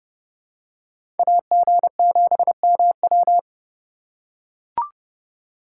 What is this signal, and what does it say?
Morse code, heard as a keyed tone, about 28 wpm, 700 Hz. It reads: AG7MW